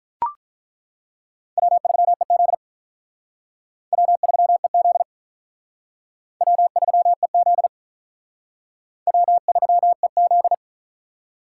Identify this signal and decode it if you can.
Morse code, heard as a keyed tone, about 42 wpm, 700 Hz. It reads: W3EZ W3EZ W3EZ W3EZ
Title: Speed Racing - USA Call Signs 35wpm